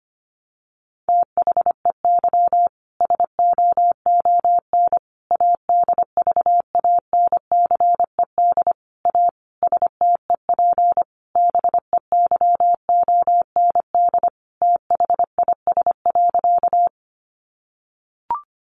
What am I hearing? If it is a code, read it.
Morse code, heard as a keyed tone, about 25 wpm, 700 Hz. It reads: THEY SOON ADVANCED A STEP BEYOND THIS.
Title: Sentences from Top 1200 Words 25wpm